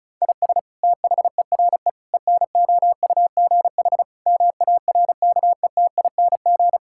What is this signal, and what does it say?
Morse code, heard as a keyed tone, about 36 wpm, 700 Hz. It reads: IS THERE ENOUGH MARKETING
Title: Sentences from Top 1000 Words - Review Part 5 - 35wpm